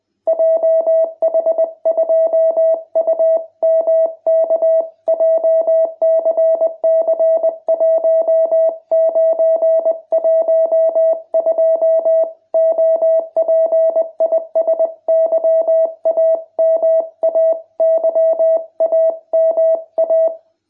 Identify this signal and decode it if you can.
Morse code, heard as a keyed tone, about 20 wpm, 660 Hz. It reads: JH2UMK JCC1912 OP IS YAMA YAMA